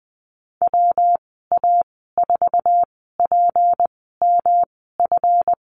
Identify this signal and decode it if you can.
Morse code, heard as a keyed tone, about 20 wpm, spaced as for 15 wpm, 700 Hz. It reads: WA4PMF
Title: 167 Lesson 27g - Number 7 - Call Signs 20wpm